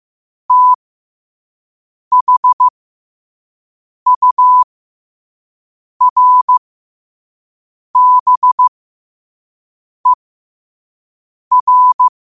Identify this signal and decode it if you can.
Morse code, heard as a keyed tone, about 15 wpm, spaced as for 5 wpm, 1000 Hz. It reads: THURBER
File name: qotdfiles / 5wpm